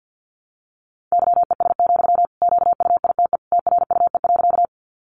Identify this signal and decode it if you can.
Morse code, heard as a keyed tone, about 50 wpm, 700 Hz. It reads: YES, QUITE TRUE.
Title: Sentences from Top 700 Words 50wpm